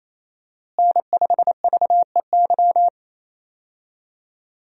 Morse code, heard as a keyed tone, about 27 wpm, 700 Hz. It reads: N5VEY
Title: USA Call Signs - 5 Characters - Repeated 3x - 28wpm